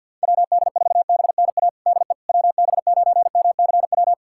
Morse code, heard as a keyed tone, about 51 wpm, 700 Hz. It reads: WD4BNA DE WB9MCW
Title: USA Callsign DE Callsign - 6 Characters - 50wpm